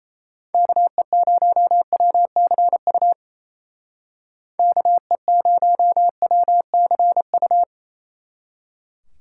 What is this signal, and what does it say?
Morse code, heard as a keyed tone, about 34 wpm, 700 Hz. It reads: KE0WCU KE0WCU
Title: Speed Racing - USA 2x3 Call Signs 28wpm